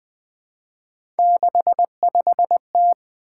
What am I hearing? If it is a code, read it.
Morse code, heard as a keyed tone, about 20 wpm, 700 Hz. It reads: 65T
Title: Three Letter-Number Combinations - 20wpm